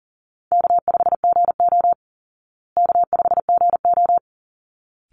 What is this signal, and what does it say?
Morse code, heard as a keyed tone, about 40 wpm, 700 Hz. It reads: K5GO K5GO
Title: Call Signs Repeated 3x - 40wpm